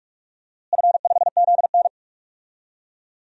Morse code, heard as a keyed tone, about 46 wpm, 700 Hz. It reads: F5ZN
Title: DXCC-100 Call Signs - 4 Characters - Repeated 3x - 45wpm